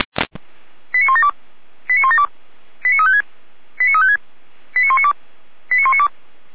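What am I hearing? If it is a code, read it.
Five-tone selective call (ZVEI), heard as quick runs of five tones, tones of about 80 ms. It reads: 89172 89172 89356 89356 89192 89192
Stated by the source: ZVEI five-tone selective call received on an SDR, from the feuerwehr-orlamuende server (fire brigade Orlamünde).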